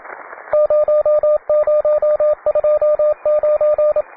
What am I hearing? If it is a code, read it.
Morse code, heard as a keyed tone, about 27 wpm, 600 Hz, with radio noise. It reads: 0029